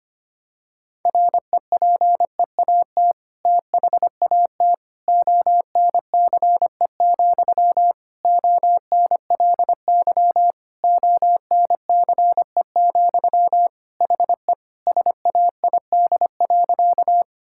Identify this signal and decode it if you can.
Morse code, heard as a keyed tone, about 25 wpm, 700 Hz. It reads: REPEAT THAT ONCE, ONLY ONCE, HE SAID.
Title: Sentences from Top 1600 Words 25wpm